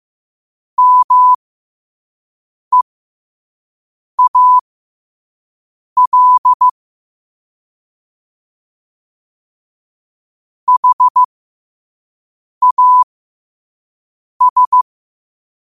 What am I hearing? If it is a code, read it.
Morse code, heard as a keyed tone, about 15 wpm, spaced as for 5 wpm, 1000 Hz. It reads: MEAL HAS
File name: qotdfiles / 5wpm